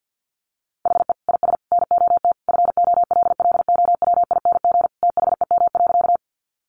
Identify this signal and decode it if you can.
Morse code, heard as a keyed tone, about 50 wpm, 700 Hz. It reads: HE IS NOT FOLLOWING THEM.
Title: Sentences from Top 700 Words 50wpm